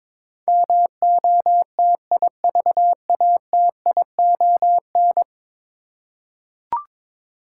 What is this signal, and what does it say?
Morse code, heard as a keyed tone, about 22 wpm, 700 Hz. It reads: MOTIVATION
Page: morse-code-ninja-practice-22wpm